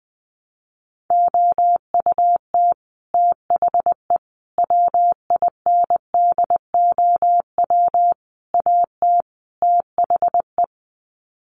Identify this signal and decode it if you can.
Morse code, heard as a keyed tone, about 20 wpm, 700 Hz. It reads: OUT THE WINDOW AT THE
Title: Sets of 5 Words Encore2 20wpm